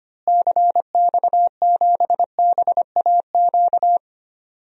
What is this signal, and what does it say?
Morse code, heard as a keyed tone, about 25 wpm, 700 Hz. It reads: CX7BAQ
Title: DXCC-100 Call Signs - 6 Characters - Repeated 2x - 25wpm